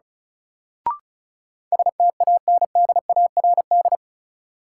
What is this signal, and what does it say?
Morse code, heard as a keyed tone, about 36 wpm, 700 Hz. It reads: STANDARD
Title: Sentences from Top 1200 Words - New Word Drill Alt - 35wpm